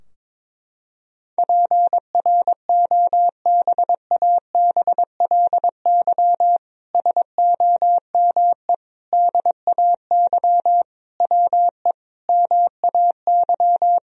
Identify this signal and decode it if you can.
Morse code, heard as a keyed tone, about 22 wpm, 700 Hz. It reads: PROBABLY SOME DAY WE MAY
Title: Sentences from Top 1000 Words - Review Part 4 - 22wpm